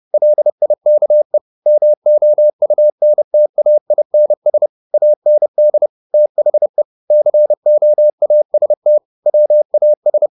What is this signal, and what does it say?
Morse code, heard as a keyed tone, about 30 wpm, 600 Hz. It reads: LIKE MOUNTAINS AND THE COAST WAS
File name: the old man and the sea / the old man and the sea 1 30wpm